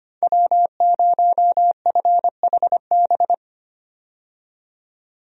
Morse code, heard as a keyed tone, about 25 wpm, 700 Hz. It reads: W0FHB
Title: USA Call Signs - 5 Characters - 25wpm